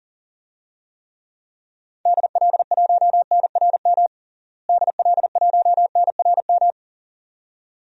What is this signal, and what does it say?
Morse code, heard as a keyed tone, about 40 wpm, 700 Hz. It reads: DL1NRM DL1NRM
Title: DXCC-100 Call Signs - 6 Characters - Repeated 2x - 40wpm